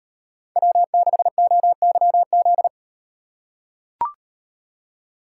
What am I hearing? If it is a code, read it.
Morse code, heard as a keyed tone, about 38 wpm, 700 Hz. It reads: W6OYZ